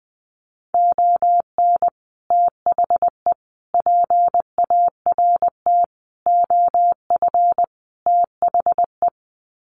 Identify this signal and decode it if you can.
Morse code, heard as a keyed tone, about 20 wpm, 700 Hz. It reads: ON THE PART OF THE
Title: Sets of 5 Words 20wpm